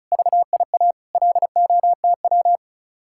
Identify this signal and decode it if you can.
Morse code, heard as a keyed tone, about 42 wpm, 700 Hz. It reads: VIA LOTW
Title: Speed-Racing Common QSO Phrases - 35wpm